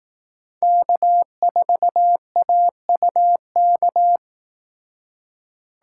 Morse code, heard as a keyed tone, about 18 wpm, 700 Hz. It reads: K4AUK
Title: Speed Racing - USA 1x3 Call Signs 15wpm